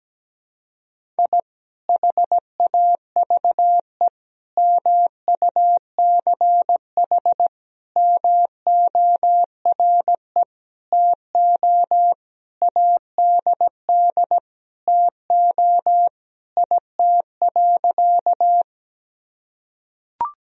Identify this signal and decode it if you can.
Morse code, heard as a keyed tone, about 17 wpm, 700 Hz. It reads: I HAVE MUCH MORE TO ADD TO IT.
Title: Sentences from Top 1000 Words - Review Part 1 - 17wpm